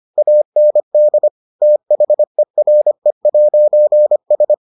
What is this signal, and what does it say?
Morse code, heard as a keyed tone, about 25 wpm, 600 Hz. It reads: AND THERE'S